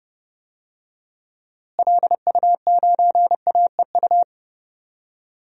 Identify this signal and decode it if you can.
Morse code, heard as a keyed tone, about 32 wpm, 700 Hz. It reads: LU9AEU
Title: DXCC-100 Call Signs - 6 Characters - Repeated 2x - 30wpm